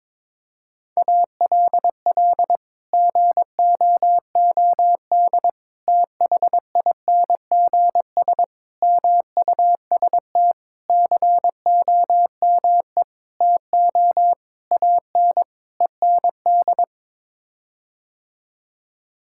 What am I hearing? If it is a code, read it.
Morse code, heard as a keyed tone, about 22 wpm, 700 Hz. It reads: ALL GOOD THINGS MUST COME TO AN END
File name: practice / 22wpm / Sets of 8 Words - 22wpm